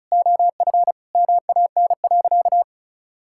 Morse code, heard as a keyed tone, about 36 wpm, 700 Hz. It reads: OF MAN.
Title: Sentences from Top 1000 Words - Review Part 6 - 35wpm